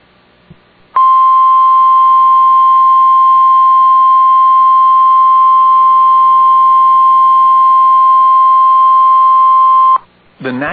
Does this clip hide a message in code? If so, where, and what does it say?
Yes: Emergency Alert System (SAME) tones (the 1050 Hz attention tone) from 0.9 to 10.0 s, with speech.